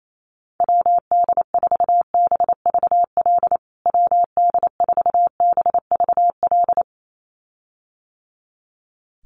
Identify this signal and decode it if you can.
Morse code, heard as a keyed tone, about 28 wpm, 700 Hz. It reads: WD4BVL WD4BVL
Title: USA Call Signs Repeated 2x 28wpm